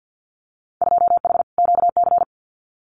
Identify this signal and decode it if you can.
Morse code, heard as a keyed tone, about 50 wpm, 700 Hz. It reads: 2H QC
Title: Winter Field Day Exchanges Encore 50wpm